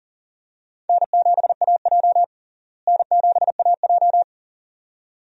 Morse code, heard as a keyed tone, about 40 wpm, 700 Hz. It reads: N7AJ N7AJ
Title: USA Call Signs - 4 Characters - Repeated 2x - 40wpm